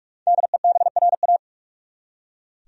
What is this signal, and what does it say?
Morse code, heard as a keyed tone, about 43 wpm, 700 Hz. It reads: DEBRA
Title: USA Common Names - 45wpm